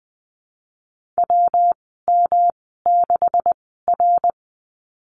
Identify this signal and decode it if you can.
Morse code, heard as a keyed tone, about 20 wpm, 700 Hz. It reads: W M 6 R
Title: 154 Lesson 25g - Letter p - Call Signs 20wpm